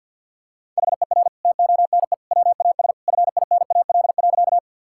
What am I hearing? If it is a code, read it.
Morse code, heard as a keyed tone, about 50 wpm, 700 Hz. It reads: HER TONE WAS FINAL.
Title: Sentences from Top 1600 Words 50wpm